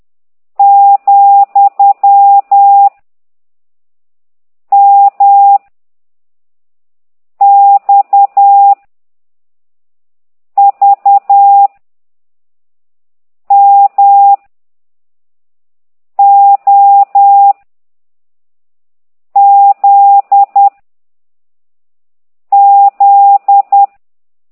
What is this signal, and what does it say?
Morse code, heard as a keyed tone, about 10 wpm, 800 Hz. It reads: , M X V M O Z Z
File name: ct2idw / exerc13